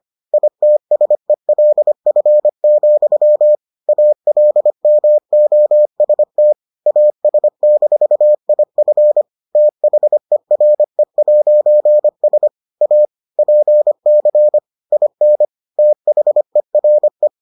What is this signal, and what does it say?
Morse code, heard as a keyed tone, about 25 wpm, 600 Hz. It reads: ITSELF, ALMOST AS-IF THERE'S A PC IN THERE